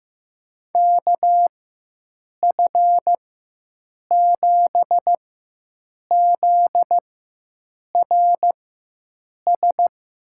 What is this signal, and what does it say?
Morse code, heard as a keyed tone, about 15 wpm, spaced as for 7 wpm, 700 Hz. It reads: KF7ZRS
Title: USA Call Signs - 6 Characters - 4x Character Spacing - 15wpm